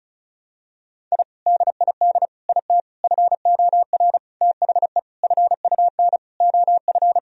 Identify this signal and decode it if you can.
Morse code, heard as a keyed tone, about 36 wpm, 700 Hz. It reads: I DID IT FOR THE FUN OF
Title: Sentences from Top 1000 Words - Review Part 4 - 35wpm